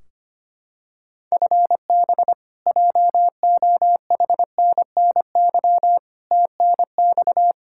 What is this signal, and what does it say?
Morse code, heard as a keyed tone, about 25 wpm, 700 Hz. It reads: FB JOHNNY TNX
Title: Practice QSO - Exchange 3 - More Info - 25wpm